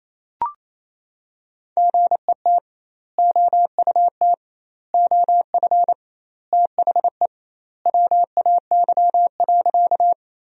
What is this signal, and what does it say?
Morse code, heard as a keyed tone, about 28 wpm, 700 Hz. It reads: GET OUT OF THE WAY.
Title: 2x Word-Spacing - Sentences from Top 200 Words - 28wpm